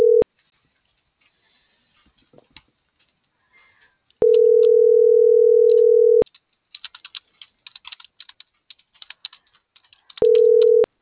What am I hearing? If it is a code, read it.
Ringback tone (North America), 440 + 480 Hz.